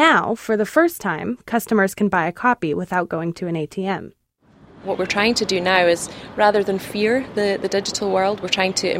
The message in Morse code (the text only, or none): none